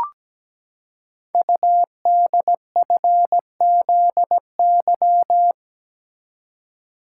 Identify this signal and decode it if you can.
Morse code, heard as a keyed tone, about 17 wpm, 700 Hz. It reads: UDFZY
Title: Code Groups - 5 Letters - 17wpm